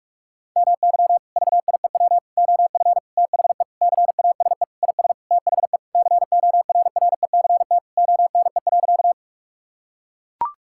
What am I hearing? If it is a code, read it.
Morse code, heard as a keyed tone, about 42 wpm, 700 Hz. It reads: MY VIEW OF THE CASE IS THE CORRECT ONE.
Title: Sentences from Top 1000 Words - Review Part 6 - 45wpm